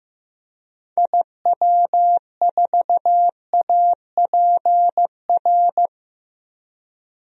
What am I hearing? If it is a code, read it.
Morse code, heard as a keyed tone, about 15 wpm, 700 Hz. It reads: IW4APR
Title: DXCC-100 Call Signs - 6 Characters - 15wpm